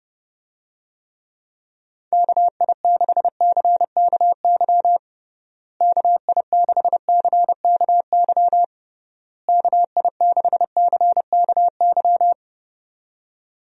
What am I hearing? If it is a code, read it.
Morse code, heard as a keyed tone, about 32 wpm, 700 Hz. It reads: KI6CKY KI6CKY KI6CKY
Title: USA Call Signs - 6 Characters - Repeated 3x - 30wpm